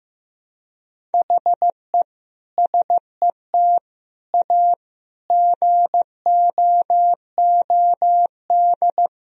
Morse code, heard as a keyed tone, about 15 wpm, 700 Hz. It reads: HE SET A GOOD
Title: Sentences from Top 1000 Words - Review Part 1 - 15wpm